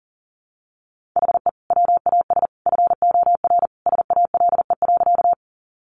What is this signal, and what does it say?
Morse code, heard as a keyed tone, about 40 wpm, 700 Hz. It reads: HE WAS FOR SALE.
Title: Sentences from Top 1400 Words 40wpm